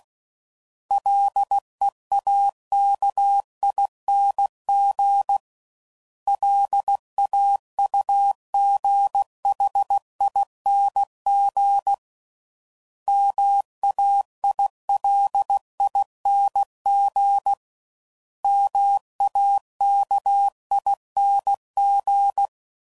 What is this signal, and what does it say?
Morse code, heard as a keyed tone, about 16 wpm, 800 Hz. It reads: LEAKING LAUGHING MAILING MAKING
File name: Suffixes / Special INGs / Sngl Ings